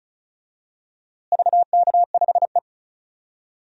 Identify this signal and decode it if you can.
Morse code, heard as a keyed tone, about 36 wpm, 700 Hz. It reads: VK5E
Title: DXCC-100 Call Signs - 4 Characters - Repeated 2x - 35wpm